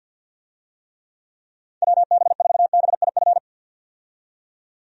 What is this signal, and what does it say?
Morse code, heard as a keyed tone, about 52 wpm, 700 Hz. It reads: WB4BIF
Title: USA Call Signs - 6 Characters - 50wpm